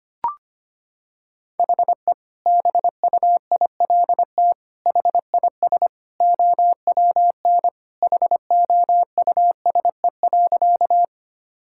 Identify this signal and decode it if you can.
Morse code, heard as a keyed tone, about 25 wpm, 700 Hz. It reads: HE BUILT HIS OWN HOUSE.
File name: practice / 25wpm / Sentences with 2-Word Phrases - 25wpm